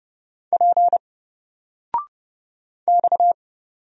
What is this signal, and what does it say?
Morse code, heard as a keyed tone, about 30 wpm, 700 Hz. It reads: P X
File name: Single-Letter-Number-30wpm